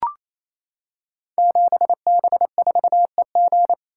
Morse code, heard as a keyed tone, about 27 wpm, 700 Hz. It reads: 7B4EG
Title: Code Groups - 5 Letter:Numbers - 28wpm